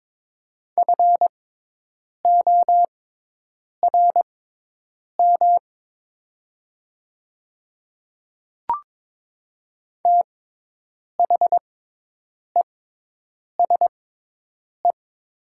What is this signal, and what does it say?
Morse code, heard as a keyed tone, about 22 wpm, spaced as for 8 wpm, 700 Hz. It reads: FORM THESE
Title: 6x Character Spacing - Top 200 Words - 22wpm